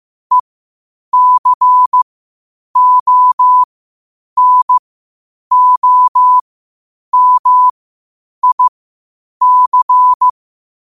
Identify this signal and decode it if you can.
Morse code, heard as a keyed tone, about 15 wpm, spaced as for 8 wpm, 1000 Hz. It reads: ECONOMIC
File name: qotdfiles / 8wpm